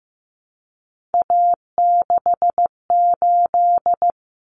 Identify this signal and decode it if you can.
Morse code, heard as a keyed tone, about 15 wpm, 700 Hz. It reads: A68
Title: Common Cut Numbers - 3-Digit 15wpm